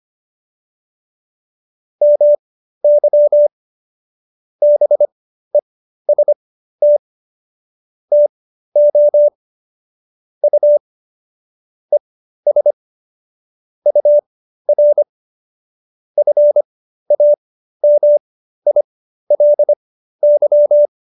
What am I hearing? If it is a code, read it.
Morse code, heard as a keyed tone, about 25 wpm, spaced as for 13 wpm, 590 Hz. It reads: MY BEST TO U ES UR FAMILY